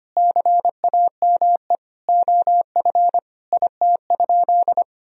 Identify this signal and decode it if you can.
Morse code, heard as a keyed tone, about 25 wpm, 700 Hz. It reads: CAME OF IT?
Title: Sentences from Top 1000 Words - Review Part 3 - 25wpm